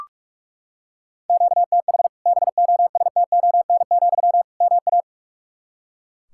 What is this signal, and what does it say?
Morse code, heard as a keyed tone, about 41 wpm, 700 Hz. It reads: QTH BOSTON, MA
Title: Practice QSO - Exchange 1 - Beginning - 45wpm